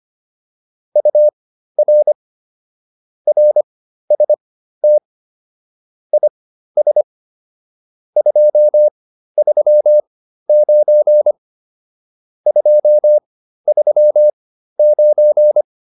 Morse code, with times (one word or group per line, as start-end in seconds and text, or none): 1.0-2.1 UR
3.3-5.0 RST
6.1-7.0 IS
8.2-11.3 239
12.5-15.6 239